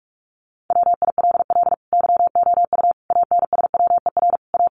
Morse code, heard as a keyed tone, about 44 wpm, 700 Hz. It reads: WILL YOU ANSWER A
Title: Sentences from Top 700 Words 45wpm